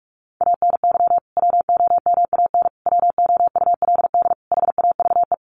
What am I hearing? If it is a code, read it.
Morse code, heard as a keyed tone, about 44 wpm, 700 Hz. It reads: ANY WOMAN WOULD HAVE
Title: Sentences from Top 1100 Words 45wpm